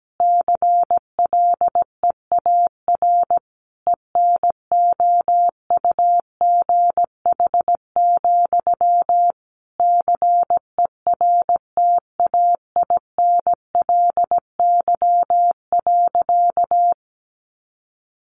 Morse code, heard as a keyed tone, about 17 wpm, 700 Hz. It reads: CLEAR ENOUGH, CERTAINLY.
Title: Sentences from Top 900 Words 17wpm